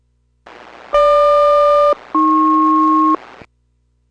Aircraft SELCAL, heard as two pairs of tones. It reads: GQ-AP